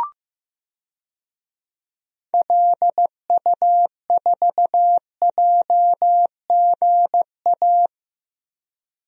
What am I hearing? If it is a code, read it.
Morse code, heard as a keyed tone, about 15 wpm, 700 Hz. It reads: LU4JGA
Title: DXCC-100 Call Signs - 6 Characters - Repeated 3x - 15wpm